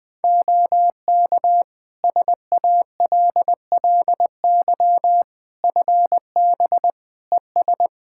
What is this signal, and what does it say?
Morse code, heard as a keyed tone, about 21 wpm, 700 Hz. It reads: OK SALLY FB ES
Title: Practice QSO - Exchange 2 - Info - 20wpm